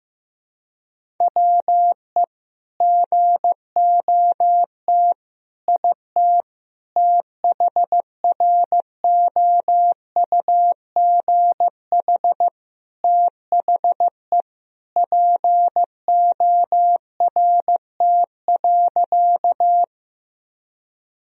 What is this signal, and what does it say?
Morse code, heard as a keyed tone, about 15 wpm, 700 Hz. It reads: WE GOT IT THROUGH THE PORT.
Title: Sentences from Top 1000 Words - Review Part 1 - 15wpm